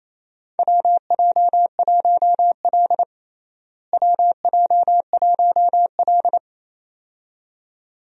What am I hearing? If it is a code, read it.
Morse code, heard as a keyed tone, about 27 wpm, 700 Hz. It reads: WJ1L WJ1L
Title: USA Call Signs - 4 Characters - Repeated 2x - 28wpm